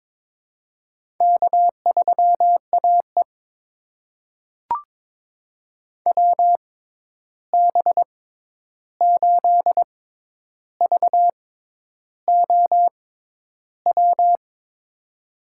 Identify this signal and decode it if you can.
Morse code, heard as a keyed tone, about 22 wpm, spaced as for 10 wpm, 700 Hz. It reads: K3AE WB8VOW
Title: USA Call Signs - 6x Character Spacing - 22wpm